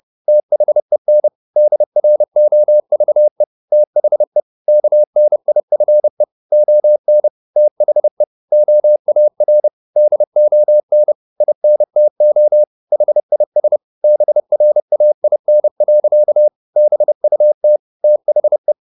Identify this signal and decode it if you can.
Morse code, heard as a keyed tone, about 30 wpm, 600 Hz. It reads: THEN DROVE THE KNIFE ON THE OAR DON INTO HIS BRAIN. BUT THE